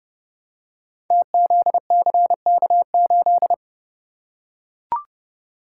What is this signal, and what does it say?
Morse code, heard as a keyed tone, about 30 wpm, 700 Hz. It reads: TZCK8
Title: Code Groups - 5 Letter:Numbers - 30wpm